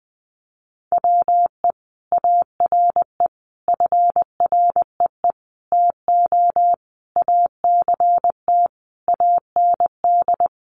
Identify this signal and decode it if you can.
Morse code, heard as a keyed tone, about 20 wpm, 700 Hz. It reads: WE ARE FREE TO ACT AND
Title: Sentences from Top 500 Words 20wpm